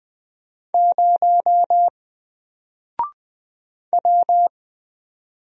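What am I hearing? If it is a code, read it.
Morse code, heard as a keyed tone, about 20 wpm, 700 Hz. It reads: 0 W
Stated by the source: Single letter and number Morse drill at 20 wpm, from N4LCD.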